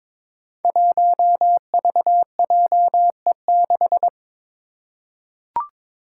Morse code, heard as a keyed tone, about 22 wpm, 700 Hz. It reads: 1VJE6
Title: Five Challenging Letter-Number Combinations 22wpm